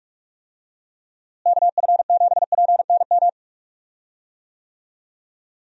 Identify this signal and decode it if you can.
Morse code, heard as a keyed tone, about 46 wpm, 700 Hz. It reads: KF7PNM
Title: USA Call Signs - 6 Characters - Repeated 2x - 45wpm